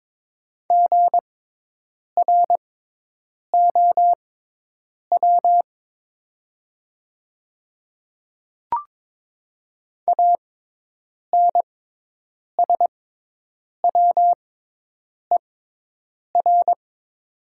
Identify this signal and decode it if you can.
Morse code, heard as a keyed tone, about 22 wpm, spaced as for 8 wpm, 700 Hz. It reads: GROW ANSWER
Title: 6x Character Spacing - Top 300 Words - 22wpm